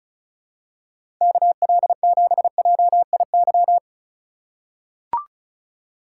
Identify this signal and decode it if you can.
Morse code, heard as a keyed tone, about 34 wpm, 700 Hz. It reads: KL7JIY
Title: USA Call Signs - 3x Character Spacing - 35wpm